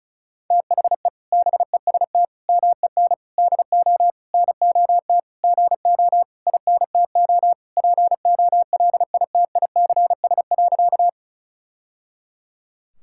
Morse code, heard as a keyed tone, about 36 wpm, 700 Hz. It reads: THE BEST MEN DO NOT GO INTO POLITICS.
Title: Sentences from Top 1000 Words - Review Part 6 - 35wpm